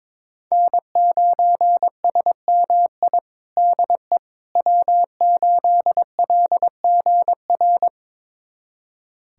Morse code, heard as a keyed tone, about 22 wpm, 700 Hz. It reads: N9SMI DE W8LGR
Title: USA Callsign DE Callsign - 5 Characters - 22wpm